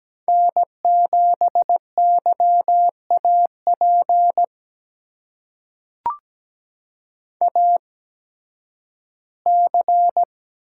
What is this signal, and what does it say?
Morse code, heard as a keyed tone, about 17 wpm, spaced as for 11 wpm, 700 Hz. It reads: N7YAP AC
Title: USA Call Signs - 5 Characters - 8x Character Spacing - 17wpm